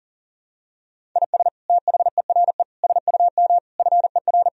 Morse code, heard as a keyed tone, about 40 wpm, 700 Hz. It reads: IS THERE SUM FER
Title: Sentences from Top 200 Words and Abbreviations 40wpm